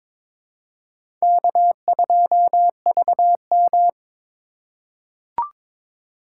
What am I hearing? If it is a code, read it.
Morse code, heard as a keyed tone, about 22 wpm, 700 Hz. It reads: K2VM
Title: USA Call Signs - 4 Characters - 2x Character Spacing - 22wpm